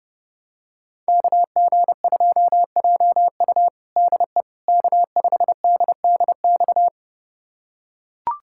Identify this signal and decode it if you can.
Morse code, heard as a keyed tone, about 32 wpm, 700 Hz. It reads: KG2JU DE K5DDX
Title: USA Callsign DE Callsign - 5 Characters - 30wpm